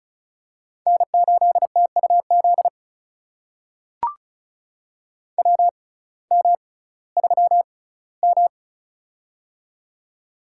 Morse code, heard as a keyed tone, about 36 wpm, spaced as for 19 wpm, 700 Hz. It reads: N8TUZ WM3M
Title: USA Call Signs - 6x Character Spacing - 35wpm